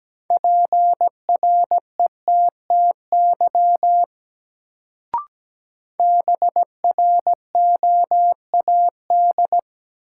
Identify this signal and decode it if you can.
Morse code, heard as a keyed tone, about 17 wpm, 700 Hz. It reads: PRETTY BROAD
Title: Sentences from Top 1000 Words - New Word Drill - 17wpm